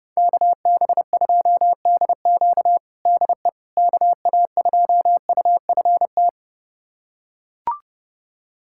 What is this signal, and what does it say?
Morse code, heard as a keyed tone, about 30 wpm, 700 Hz. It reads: KB2DQ DE KA2UFT
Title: USA Callsign DE Callsign 30wpm